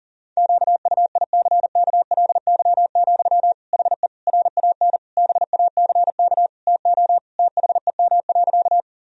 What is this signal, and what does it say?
Morse code, heard as a keyed tone, about 40 wpm, 700 Hz. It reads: QUICKLY, HE RAN BACK TO THEM.
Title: Sentences from Top 1000 Words - Review Part 3 - 40wpm